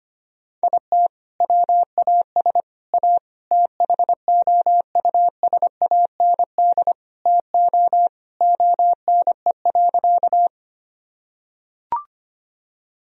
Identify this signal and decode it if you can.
Morse code, heard as a keyed tone, about 25 wpm, 700 Hz. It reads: IT WAS A THOUSAND TO ONE.
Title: Sentences from Top 1000 Words - Review Part 3 - 25wpm